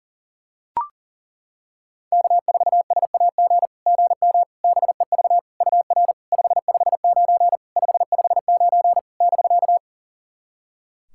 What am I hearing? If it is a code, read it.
Morse code, heard as a keyed tone, about 40 wpm, 700 Hz. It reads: K4SAG GM BEV UR 559 559 <BK>
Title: SOTA QSO Practice 40wpm